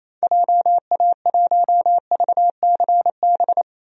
Morse code, heard as a keyed tone, about 28 wpm, 700 Hz. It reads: JA1VCB
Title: DXCC-100 Call Signs - 6 Characters - 28wpm